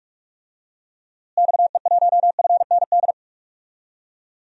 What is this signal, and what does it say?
Morse code, heard as a keyed tone, about 44 wpm, 700 Hz. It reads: XE1FND
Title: DXCC-100 Call Signs - 6 Characters - Repeated 3x - 45wpm